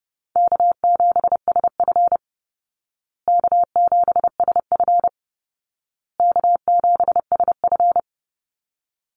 Morse code, heard as a keyed tone, about 30 wpm, 700 Hz. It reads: K7SF K7SF K7SF
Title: Easy Call Signs Repeated 3x - 30wpm